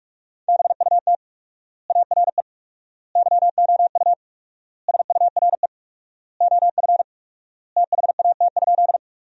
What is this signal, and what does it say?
Morse code, heard as a keyed tone, about 46 wpm, 700 Hz. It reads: BUT ARE YOU SURE OF THAT?
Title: 4x Word-Spacing - Sentences from Top 300 Words - 45wpm